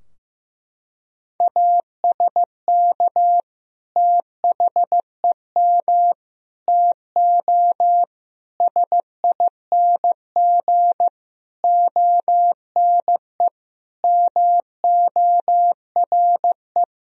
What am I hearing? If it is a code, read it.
Morse code, heard as a keyed tone, about 15 wpm, 700 Hz. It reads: ASK THEM TO SING ONE MORE
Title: Sentences from Top 1000 Words - Review Part 2 - 15wpm